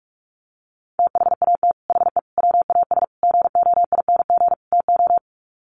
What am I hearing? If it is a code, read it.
Morse code, heard as a keyed tone, about 44 wpm, 700 Hz. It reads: THAT HE WAS GOING TO
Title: Sets of 5 Words 45wpm